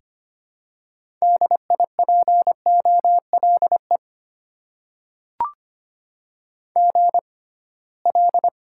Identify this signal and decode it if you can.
Morse code, heard as a keyed tone, about 25 wpm, spaced as for 19 wpm, 690 Hz. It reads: DIPOLE GL
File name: Top 90 QSO Elements - 6x Character Spacing - 25wpm